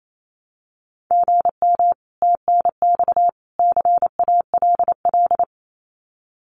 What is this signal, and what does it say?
Morse code, heard as a keyed tone, about 28 wpm, 700 Hz. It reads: GM TNX CALL
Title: Common QSO Phrases 28wpm